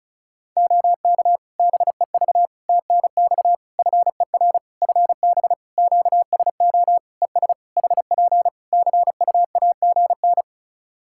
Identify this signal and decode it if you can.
Morse code, heard as a keyed tone, about 36 wpm, 700 Hz. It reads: OK BEV TNX FER FB QSO ES HP CUAGN
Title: Practice QSO - Exchange 4 - Ending - 35wpm